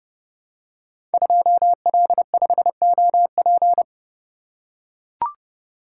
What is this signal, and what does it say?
Morse code, heard as a keyed tone, about 30 wpm, 700 Hz. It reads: 2L5OP
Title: Code Groups - 5 Letter:Numbers - 30wpm